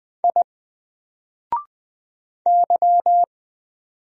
Morse code, heard as a keyed tone, about 20 wpm, 700 Hz. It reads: I Y
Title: Single Character (Letter:Number:Punctuation:Prosign) Alt 20wpm